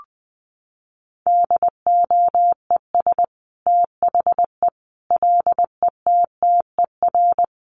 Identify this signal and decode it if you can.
Morse code, heard as a keyed tone, about 20 wpm, 700 Hz. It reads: DOES THE LETTER
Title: Sentences from Top 1100 Words 20wpm